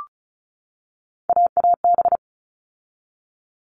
Morse code, heard as a keyed tone, about 34 wpm, 700 Hz. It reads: AAB
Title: All Cut Numbers - 3-Digit 35wpm